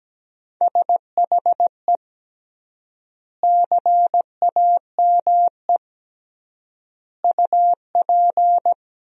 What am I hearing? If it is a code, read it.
Morse code, heard as a keyed tone, about 17 wpm, 700 Hz. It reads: SHE CAME UP
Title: 3x Word-Spacing - Sentences from Top 200 Words - 17wpm